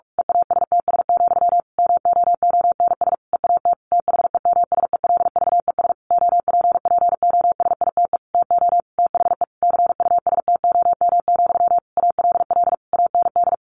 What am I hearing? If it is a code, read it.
Morse code, heard as a keyed tone, about 44 wpm, 700 Hz. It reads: EASTS, MOONS EAT THEMSELVES OPPOSITE TO THE CUSTOM, ALL AND
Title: A Sound Of Thunder 45wpm